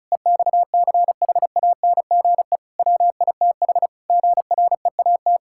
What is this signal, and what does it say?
Morse code, heard as a keyed tone, about 36 wpm, 700 Hz. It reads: EXCHANGE WITH GREAT
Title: Sentences from Top 1000 Words - Review Part 6 - 35wpm